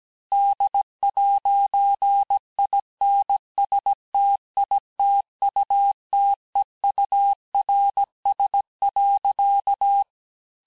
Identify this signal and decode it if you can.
Morse code, heard as a keyed tone, about 17 wpm, 800 Hz. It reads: D'INSTITUTEURS.
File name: texte clair vitesse 16 puis 21 mpm